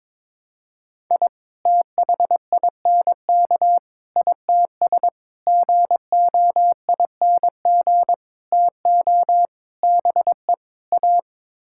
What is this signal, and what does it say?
Morse code, heard as a keyed tone, about 22 wpm, 700 Hz. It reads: I THINK ITS GOING TO BE A
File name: practice / 22wpm / Sets of 7 Words - 22wpm